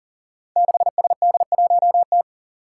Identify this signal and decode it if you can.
Morse code, heard as a keyed tone, about 40 wpm, 700 Hz. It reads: 6SD1T
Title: Five Challenging Letter-Number Combinations 40wpm